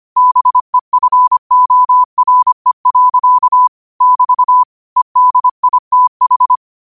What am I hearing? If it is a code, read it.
Morse code, heard as a keyed tone, about 25 wpm, 1000 Hz. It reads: BEFORE. <BT> EDITH